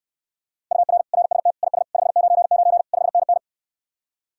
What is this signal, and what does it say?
Morse code, heard as a keyed tone, about 67 wpm, 700 Hz. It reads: UR RST IS 599 5NN